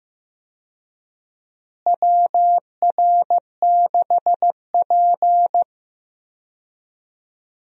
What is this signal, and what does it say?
Morse code, heard as a keyed tone, about 15 wpm, 700 Hz. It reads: WR6P
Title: USA Call Signs - 4 Characters - Repeated 2x - 15wpm